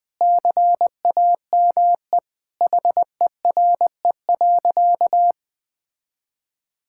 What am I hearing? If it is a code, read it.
Morse code, heard as a keyed tone, about 20 wpm, 700 Hz. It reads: CAME HERE.
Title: Sentences with 2-Word Phrases - 20wpm